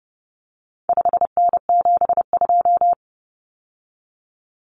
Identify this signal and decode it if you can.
Morse code, heard as a keyed tone, about 30 wpm, 700 Hz. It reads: 5N72
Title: Common Cut Numbers - 4-Digit 30wpm